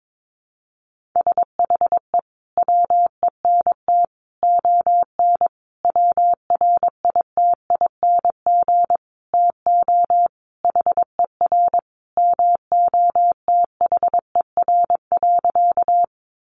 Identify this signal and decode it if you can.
Morse code, heard as a keyed tone, about 22 wpm, 700 Hz. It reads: SHE WENT ON WRITING TO HER MOTHER.
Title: Sentences from Top 800 Words 22wpm